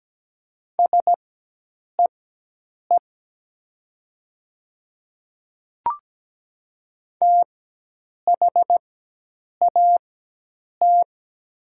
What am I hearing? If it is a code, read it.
Morse code, heard as a keyed tone, about 17 wpm, spaced as for 8 wpm, 700 Hz. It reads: SEE THAT
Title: 4x Character Spacing - Top 100 Words - 17wpm